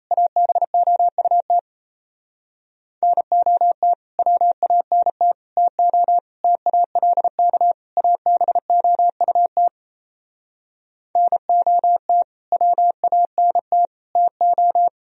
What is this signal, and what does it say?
Morse code, heard as a keyed tone, about 38 wpm, 700 Hz. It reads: ABOUT NOT WANT TO TALK ABOUT NOT WANT TO